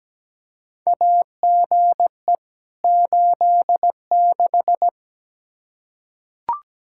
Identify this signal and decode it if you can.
Morse code, heard as a keyed tone, about 17 wpm, 700 Hz. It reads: AGE 86
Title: K1USN SST QSO Practice 17wpm